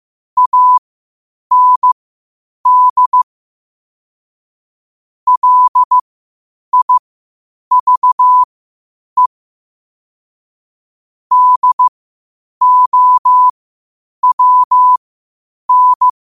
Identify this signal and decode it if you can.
Morse code, heard as a keyed tone, about 15 wpm, spaced as for 8 wpm, 1000 Hz. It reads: AND LIVE DOWN
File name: 8wpm